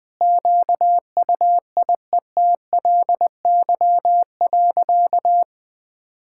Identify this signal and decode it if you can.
Morse code, heard as a keyed tone, about 21 wpm, 700 Hz. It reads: QUIETLY.
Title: Sentences with 2-Word Phrases - Spoken Beginning - 20wpm